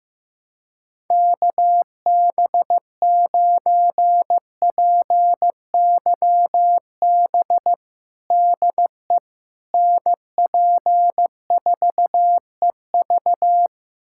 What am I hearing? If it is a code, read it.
Morse code, heard as a keyed tone, about 16 wpm, 700 Hz. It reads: KB9PYB DE NP4EV
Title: USA Callsign DE Callsign 15wpm